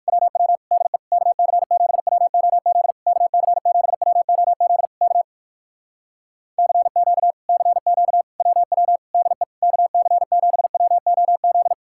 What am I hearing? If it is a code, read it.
Morse code, heard as a keyed tone, about 59 wpm, 700 Hz. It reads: WW DE KC7WOZ KC7WOZ K CQ CQ WW DE KC7WOZ